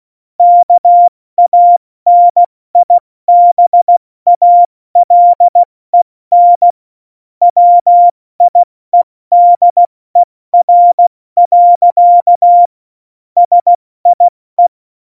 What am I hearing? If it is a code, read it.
Morse code, heard as a keyed tone, about 16 wpm, 700 Hz. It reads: KANIBALEN WIEDER. SIE
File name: Morsekurs / Witze / Witz 12. Januar 2021, 70 F80